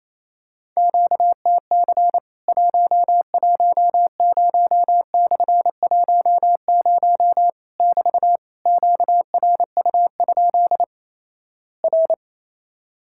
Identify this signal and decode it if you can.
Morse code, heard as a keyed tone, about 28 wpm, 700 Hz. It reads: QTC 110/10 <BT> QRU? R
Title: Worked All Europe DX Contest - QTC Exchange - 28wpm